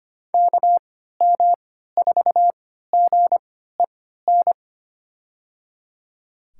Morse code, heard as a keyed tone, about 25 wpm, spaced as for 14 wpm, 700 Hz. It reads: KM4GEN
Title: USA Call Signs - 3x Character Spacing - 25wpm